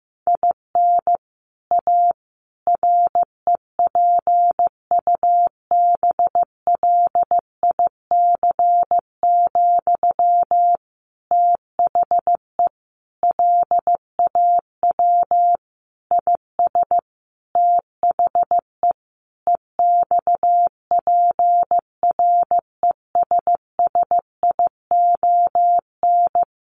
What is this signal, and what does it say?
Morse code, heard as a keyed tone, about 15 wpm, 700 Hz. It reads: IN A REPUBLIC, THE LAW IS THE EXPRESSION
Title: Sentences from Top 1500 Words 15wpm